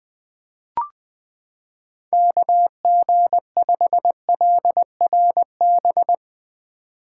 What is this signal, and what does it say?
Morse code, heard as a keyed tone, about 20 wpm, 700 Hz. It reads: KG5LRB
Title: 2023 POTA Activator Call Signs - 20wpm